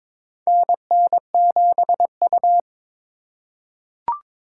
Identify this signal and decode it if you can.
Morse code, heard as a keyed tone, about 23 wpm, 700 Hz. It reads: NN7U
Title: USA Call Signs - 4 Characters - 6x Character Spacing - 22wpm